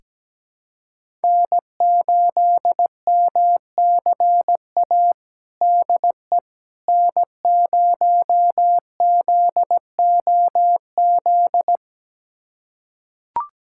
Morse code, heard as a keyed tone, about 17 wpm, 700 Hz. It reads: N8MCA DE N0ZOZ
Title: USA Callsign DE Callsign - 5 Characters - 17wpm